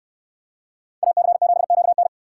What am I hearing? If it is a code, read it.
Morse code, heard as a keyed tone, about 62 wpm, 700 Hz. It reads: AC7QN